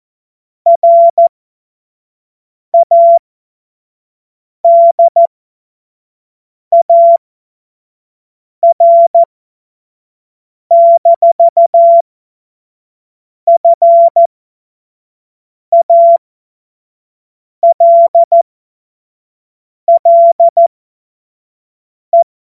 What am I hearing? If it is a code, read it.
Morse code, heard as a keyed tone, about 14 wpm, spaced as for 5 wpm, 680 Hz. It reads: RADAR-FALLE
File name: Witz-05-August-2025-25-F70